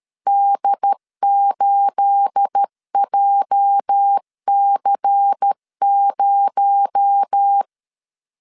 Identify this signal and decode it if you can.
Morse code, heard as a keyed tone, about 13 wpm, 800 Hz. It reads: D8JC0
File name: CW / DET / 1 9 1500C S C 12wpm 3811